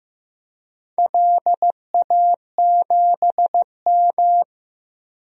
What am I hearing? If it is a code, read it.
Morse code, heard as a keyed tone, about 15 wpm, 700 Hz. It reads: LA7M
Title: DXCC-100 Call Signs - 4 Characters - 15wpm